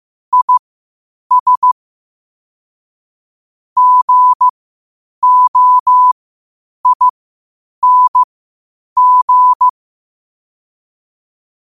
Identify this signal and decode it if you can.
Morse code, heard as a keyed tone, about 15 wpm, spaced as for 8 wpm, 1000 Hz. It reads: IS GOING
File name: qotdfiles / 8wpm